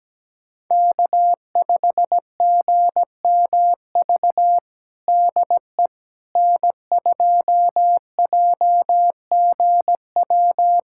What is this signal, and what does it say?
Morse code, heard as a keyed tone, about 17 wpm, 700 Hz. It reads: K5GMV DE N2JGW
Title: USA Callsign DE Callsign - 5 Characters - 17wpm